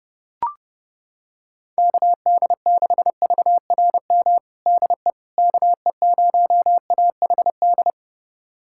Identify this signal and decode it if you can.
Morse code, heard as a keyed tone, about 30 wpm, 700 Hz. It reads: KD6VRM DE KE0AHD
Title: USA Callsign DE Callsign 30wpm